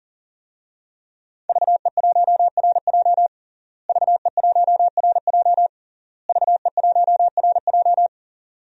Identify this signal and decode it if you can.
Morse code, heard as a keyed tone, about 40 wpm, 700 Hz. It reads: VE1RJ VE1RJ VE1RJ
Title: DXCC-100 Call Signs - 5 Characters - Repeated 3x - 40wpm